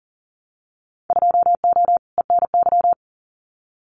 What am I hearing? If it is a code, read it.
Morse code, heard as a keyed tone, about 40 wpm, 700 Hz. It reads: 2O ENY
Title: Winter Field Day Exchanges 40wpm